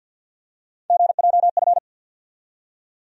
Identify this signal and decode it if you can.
Morse code, heard as a keyed tone, about 50 wpm, 700 Hz. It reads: GJF
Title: USA Call Sign Postfixes - 3 Letters 50wpm